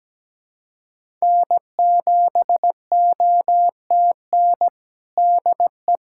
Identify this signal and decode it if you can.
Morse code, heard as a keyed tone, about 17 wpm, 700 Hz. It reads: N7OTN DE
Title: USA Callsign DE Callsign - 5 Characters - 17wpm